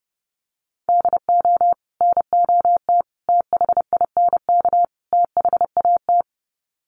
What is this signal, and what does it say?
Morse code, heard as a keyed tone, about 30 wpm, 700 Hz. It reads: DO NOT THINK THAT
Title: Sets of 4 Words 30wpm